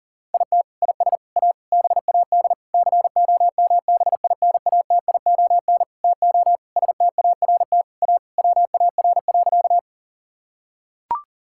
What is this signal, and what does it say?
Morse code, heard as a keyed tone, about 40 wpm, 700 Hz. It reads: IT IS A BAD COMBINATION TO START A WAR.
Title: Sentences from Top 1000 Words - Review Part 6 - 40wpm